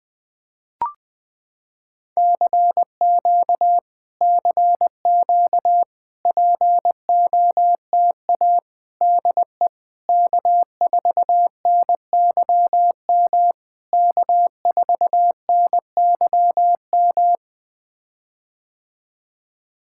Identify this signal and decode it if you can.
Morse code, heard as a keyed tone, about 21 wpm, 700 Hz. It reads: CQ CQ POTA DE K4NYM K4NYM
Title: POTA QSO Practice - 20wpm